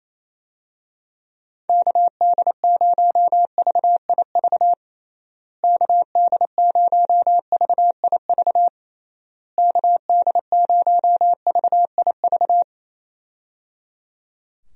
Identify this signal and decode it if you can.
Morse code, heard as a keyed tone, about 27 wpm, 700 Hz. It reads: KD0VIV KD0VIV KD0VIV
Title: USA Call Signs - 6 Characters - Repeated 3x - 28wpm